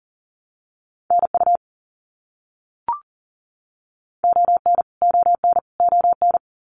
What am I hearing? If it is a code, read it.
Morse code, heard as a keyed tone, about 40 wpm, 700 Hz. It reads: NU ON ON ON
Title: Canadian Provinces and Territories - Repeated 3x - 40wpm